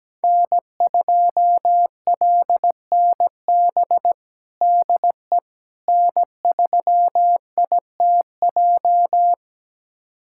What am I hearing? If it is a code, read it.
Morse code, heard as a keyed tone, about 17 wpm, 700 Hz. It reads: N2LNB DE N3ITJ
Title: USA Callsign DE Callsign - 5 Characters - 17wpm